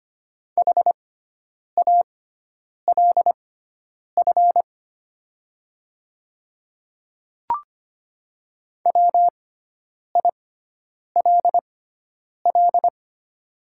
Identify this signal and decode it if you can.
Morse code, heard as a keyed tone, about 25 wpm, spaced as for 9 wpm, 700 Hz. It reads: HALF WILL
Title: 6x Character Spacing - Top 500 Words - 25wpm